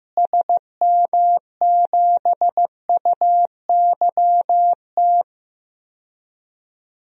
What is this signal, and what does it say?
Morse code, heard as a keyed tone, about 15 wpm, 700 Hz. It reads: SM7UYT
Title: DXCC-100 Call Signs - 6 Characters - Repeated 2x - 15wpm